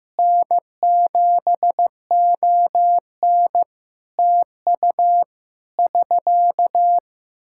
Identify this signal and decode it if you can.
Morse code, heard as a keyed tone, about 15 wpm, 700 Hz. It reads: N7ON TU <SK>
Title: Practice QSO - Exchange 4 - Ending - 15wpm